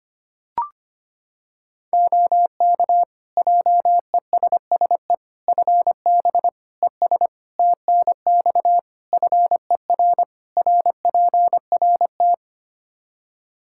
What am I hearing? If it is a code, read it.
Morse code, heard as a keyed tone, about 25 wpm, 700 Hz. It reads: OK JESSE FB ES TNX FER RPRT
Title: Practice QSO - Exchange 2 - Info - 25wpm